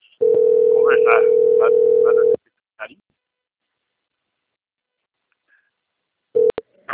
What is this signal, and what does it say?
Ringback tone (North America), 440 + 480 Hz, with speech.
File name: out- - - - - .26203